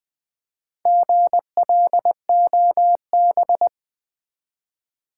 Morse code, heard as a keyed tone, about 20 wpm, 700 Hz. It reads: GLOB
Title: Build A Word - 1 Letter At A Time - 5-Letter Words - 20wpm